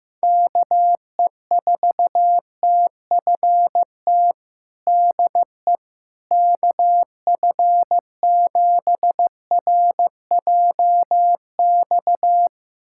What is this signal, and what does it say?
Morse code, heard as a keyed tone, about 16 wpm, 700 Hz. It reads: KE4TFT DE KF7RJX
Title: USA Callsign DE Callsign - 6 Characters - 15wpm